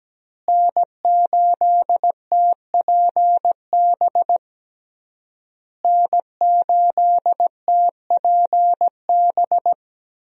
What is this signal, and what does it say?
Morse code, heard as a keyed tone, about 17 wpm, 700 Hz. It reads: N8TPB N8TPB
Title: USA Call Signs - 5 Characters - Repeated 3x - 17wpm